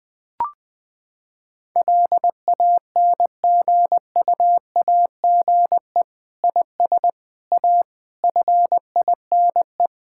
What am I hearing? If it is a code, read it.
Morse code, heard as a keyed tone, about 20 wpm, 700 Hz. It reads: LANGUAGE IS A FINE
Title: Sentences from Top 1000 Words - Review Part 6 - 20wpm